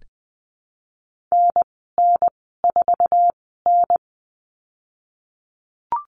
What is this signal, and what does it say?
Morse code, heard as a keyed tone, about 20 wpm, spaced as for 15 wpm, 700 Hz. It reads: NN4N
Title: 055 Lesson 10g - Letter r - Call Signs 20wpm